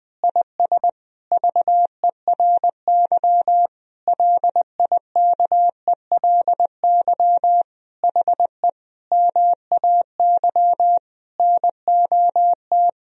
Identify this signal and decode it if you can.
Morse code, heard as a keyed tone, about 20 wpm, 700 Hz. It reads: IS VERY LIKELY HE MAY NOT
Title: Sentences from Top 1000 Words - Review Part 4 - 20wpm